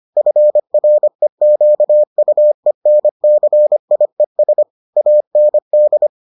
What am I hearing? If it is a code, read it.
Morse code, heard as a keyed tone, about 25 wpm, 600 Hz. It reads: FREQUENCIES AND